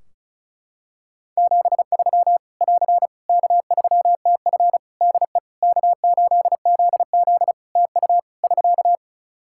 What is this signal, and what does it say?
Morse code, heard as a keyed tone, about 36 wpm, 700 Hz. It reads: 73 <AR> K3TF DE K8ZZ TU <SK>
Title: K1USN SST QSO Practice 35wpm